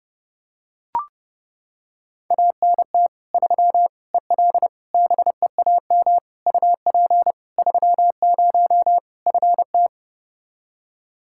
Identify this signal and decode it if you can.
Morse code, heard as a keyed tone, about 30 wpm, 700 Hz. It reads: ANT 3 EL BEAM UP 30 FT
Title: Practice QSO - Exchange 2 - Info - 30wpm